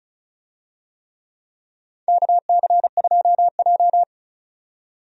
Morse code, heard as a keyed tone, about 36 wpm, 700 Hz. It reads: KC2J
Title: USA Call Signs - 4 Characters - 35wpm